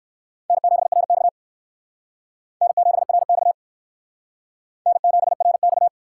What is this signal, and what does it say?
Morse code, heard as a keyed tone, about 57 wpm, 700 Hz. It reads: N7RX N7RX N7RX